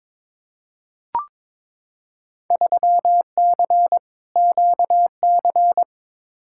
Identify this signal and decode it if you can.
Morse code, heard as a keyed tone, about 22 wpm, 700 Hz. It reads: 3C QC
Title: Field Day Exchanges Encore - 22wpm